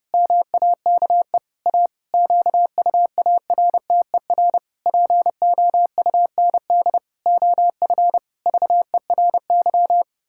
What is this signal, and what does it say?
Morse code, heard as a keyed tone, about 30 wpm, 700 Hz. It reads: MAKE A QUARTER POUND OF VERY
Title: Pangram Challenge v2 - 30wpm